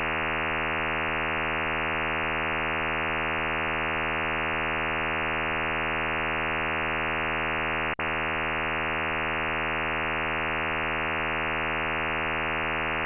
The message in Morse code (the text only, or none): none